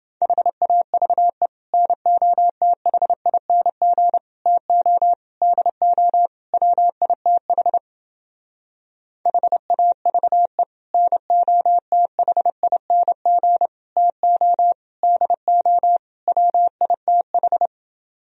Morse code, heard as a keyed tone, about 28 wpm, 700 Hz. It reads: HAVE NOTHING TO DO WITH HAVE NOTHING TO DO WITH